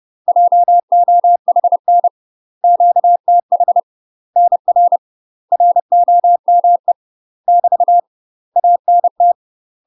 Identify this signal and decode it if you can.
Morse code, heard as a keyed tone, about 30 wpm, 700 Hz. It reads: JOHN QTH NR ROME <BT> ANT